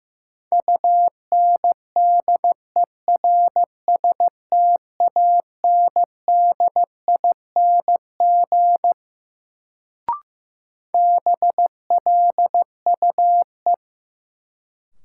Morse code, heard as a keyed tone, about 15 wpm, 700 Hz. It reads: UNDERSTANDING BLUE
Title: Sentences from Top 1000 Words - Review Part 3 - New Word Drill - 15wpm